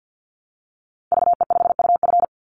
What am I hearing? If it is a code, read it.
Morse code, heard as a keyed tone, about 49 wpm, 700 Hz. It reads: VE5UF
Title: Call Signs Repeated 3x - 50wpm